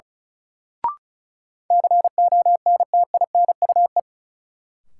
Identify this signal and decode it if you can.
Morse code, heard as a keyed tone, about 34 wpm, 700 Hz. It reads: CONTINUE
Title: Sentences from Top 1100 Words - New Word Drill Alt - 35wpm